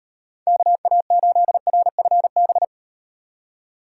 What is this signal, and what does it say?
Morse code, heard as a keyed tone, about 34 wpm, 700 Hz. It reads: KA8RFB